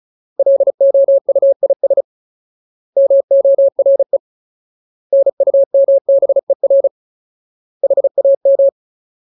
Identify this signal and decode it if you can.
Morse code, heard as a keyed tone, about 34 wpm, 550 Hz. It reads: LOUIS MORE NUMBER HAM
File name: Ultimate Word List / 35WPM Ultimate